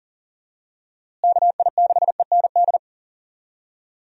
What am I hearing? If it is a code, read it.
Morse code, heard as a keyed tone, about 40 wpm, 700 Hz. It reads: KI6END
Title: USA Call Signs - 6 Characters - Repeated 3x - 40wpm